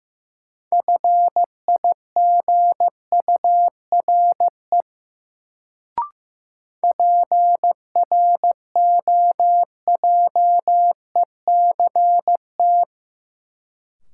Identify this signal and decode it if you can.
Morse code, heard as a keyed tone, about 15 wpm, 700 Hz. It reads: FIGURE PROJECT
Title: Sentences from Top 1400 Words - New Word Drill Alt - 15wpm